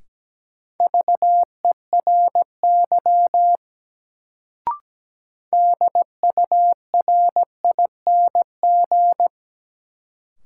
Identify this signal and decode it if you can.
Morse code, heard as a keyed tone, about 17 wpm, 700 Hz. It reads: VERY DURING
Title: Sets of 2 Words - New Word Drill - 17wpm